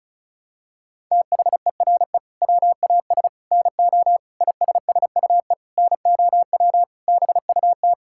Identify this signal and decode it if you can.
Morse code, heard as a keyed tone, about 36 wpm, 700 Hz. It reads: THERE WAS NO ISSUE NOW BUT
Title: Sentences from Top 1000 Words - Review Part 6 - 35wpm